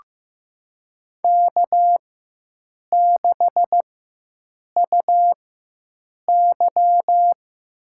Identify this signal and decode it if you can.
Morse code, heard as a keyed tone, about 16 wpm, spaced as for 7 wpm, 700 Hz. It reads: K6UY
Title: USA Call Signs - 4 Characters - 4x Character Spacing - 15wpm